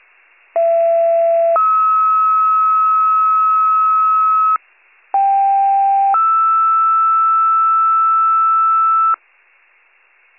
Two-tone page, heard as a one-second tone, then a three-second tone, about 670 then 1290 Hz, about 790 then 1320 Hz. It is heard from 0.5 to 9.1 s.